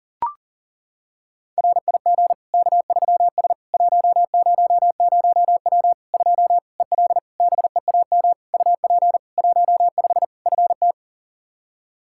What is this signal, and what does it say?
Morse code, heard as a keyed tone, about 37 wpm, 700 Hz. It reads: RIG K3S 100W 2 EL BEAM UP 15 FT
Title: Practice QSO - Exchange 2 - Info - 40wpm